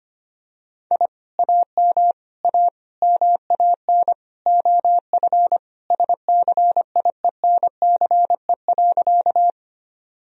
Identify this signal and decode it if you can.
Morse code, heard as a keyed tone, about 25 wpm, 700 Hz. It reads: I AM A MAN OF SCIENCE.
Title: Sentences from Top 1000 Words - Review Part 2 - 25wpm